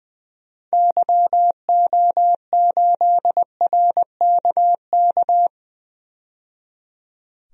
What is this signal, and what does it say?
Morse code, heard as a keyed tone, about 20 wpm, 700 Hz. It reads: YO8RKK
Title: DXCC-100 Call Signs - 6 Characters - 20wpm